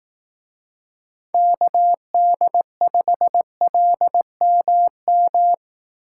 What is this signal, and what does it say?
Morse code, heard as a keyed tone, about 18 wpm, 700 Hz. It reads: KD5LMM
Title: Speed Racing - USA 2x3 Call Signs 15wpm